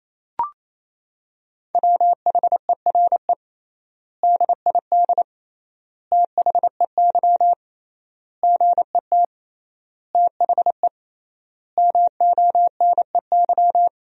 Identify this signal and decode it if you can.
Morse code, heard as a keyed tone, about 28 wpm, 700 Hz. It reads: WHERE DID THEY GET THE MONEY
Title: 3x Word-Spacing - Sentences from Top 400 Words - 28wpm